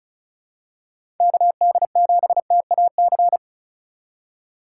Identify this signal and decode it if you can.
Morse code, heard as a keyed tone, about 36 wpm, 700 Hz. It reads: KD7TAC
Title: USA Call Signs - 6 Characters - 4x Character Spacing - 35wpm